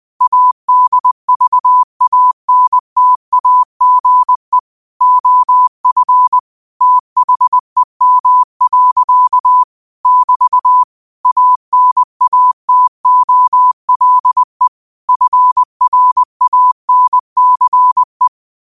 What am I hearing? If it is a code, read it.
Morse code, heard as a keyed tone, about 20 wpm, 1000 Hz. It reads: ADVANTAGE OF THEM. <BT> ANATOLE FRANCE
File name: qotdfiles / 20wpm